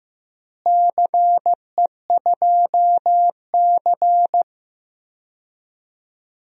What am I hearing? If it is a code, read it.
Morse code, heard as a keyed tone, about 16 wpm, 700 Hz. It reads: CE2C
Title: DXCC-100 Call Signs - 4 Characters - Repeated 2x - 15wpm